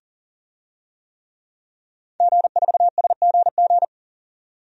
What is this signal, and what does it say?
Morse code, heard as a keyed tone, about 40 wpm, 700 Hz. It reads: G4SGG
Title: DXCC-100 Call Signs - 5 Characters - 40wpm